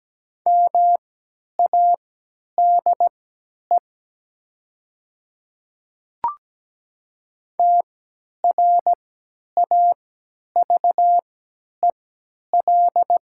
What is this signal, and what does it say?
Morse code, heard as a keyed tone, about 17 wpm, spaced as for 10 wpm, 700 Hz. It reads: MADE TRAVEL
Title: 3x Character Spacing - Top 500 Words - 17wpm